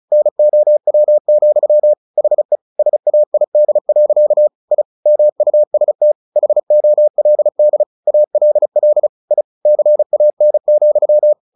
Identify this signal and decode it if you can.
Morse code, heard as a keyed tone, about 34 wpm, 600 Hz. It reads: NOW, HE SAID. I MUST HOLD ALL I CAN,
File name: the old man and the sea 2 35wpm